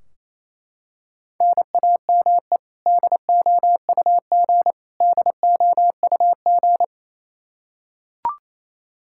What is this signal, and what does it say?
Morse code, heard as a keyed tone, about 28 wpm, 700 Hz. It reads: NAME DOUG DOUG
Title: Practice QSO - Exchange 1 - Beginning - 28wpm